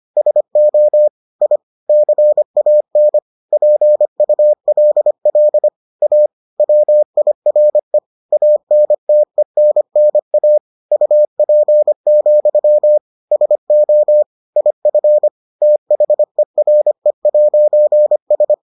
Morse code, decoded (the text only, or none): SO I CAN PULL A WIRE ANTENNA UP, SO IF THERE'S